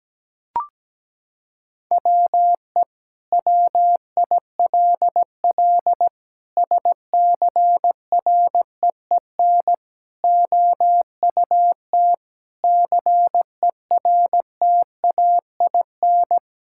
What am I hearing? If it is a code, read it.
Morse code, heard as a keyed tone, about 17 wpm, 700 Hz. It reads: WE WILL SCREEN OUT CERTAIN
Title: Sentences from Top 1000 Words - Review Part 6 - 17wpm